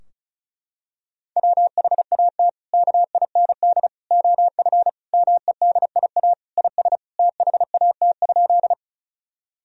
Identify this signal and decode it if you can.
Morse code, heard as a keyed tone, about 36 wpm, 700 Hz. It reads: WHAT KIND OF MEDIA IS THAT?
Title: Sentences from Top 1000 Words - Review Part 4 - 35wpm